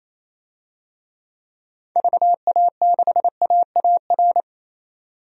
Morse code, heard as a keyed tone, about 28 wpm, 700 Hz. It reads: VA6AAR
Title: DXCC-100 Call Signs - 6 Characters - 28wpm